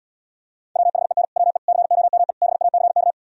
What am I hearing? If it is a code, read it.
Morse code, heard as a keyed tone, about 77 wpm, 700 Hz. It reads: WHEN WE COME BACK